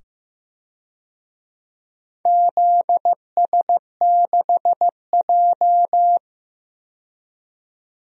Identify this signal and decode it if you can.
Morse code, heard as a keyed tone, about 15 wpm, 700 Hz. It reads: ZS6J
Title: DXCC-100 Call Signs - 4 Characters - 15wpm